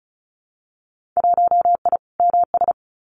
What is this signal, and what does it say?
Morse code, heard as a keyed tone, about 34 wpm, 700 Hz. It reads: 1I MS
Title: Winter Field Day Exchanges Encore 35wpm